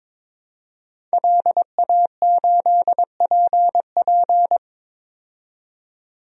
Morse code, heard as a keyed tone, about 22 wpm, 700 Hz. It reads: LA8PP
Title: DXCC-100 Call Signs - 5 Characters - 22wpm